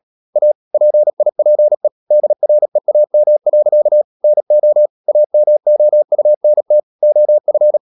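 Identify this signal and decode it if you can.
Morse code, heard as a keyed tone, about 37 wpm, 600 Hz. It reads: A PIPE DREAM. NO AMOUNT OF
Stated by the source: Morse code practice at 35–39 wpm sending English text of Ken Follett's novel Never, from CW med Gnister.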